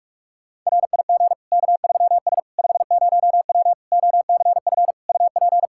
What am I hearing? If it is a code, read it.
Morse code, heard as a keyed tone, about 46 wpm, 700 Hz. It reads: RIG K3S 50W OCF UP
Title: Practice QSO - Exchange 2 - Info - 45wpm